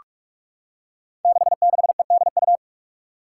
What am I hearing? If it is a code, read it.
Morse code, heard as a keyed tone, about 46 wpm, 700 Hz. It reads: 66EDU
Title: Five Challenging Letter-Number Combinations 45wpm